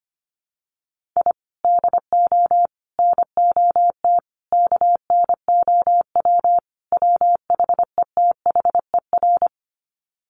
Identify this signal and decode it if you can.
Morse code, heard as a keyed tone, about 25 wpm, 700 Hz. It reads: I DO NOT KNOW WHETHER
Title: Sets of 5 Words 25wpm